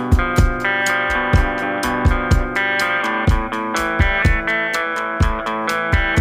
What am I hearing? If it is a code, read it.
Emergency Alert System (SAME) tones (header bursts), heard as harsh buzzing bursts. It is heard from 0.6 to 5.1 s.